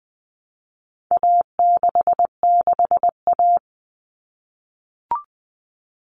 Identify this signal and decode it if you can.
Morse code, heard as a keyed tone, about 20 wpm, 700 Hz. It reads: A66A
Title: All Cut Numbers - 4-Digit 20wpm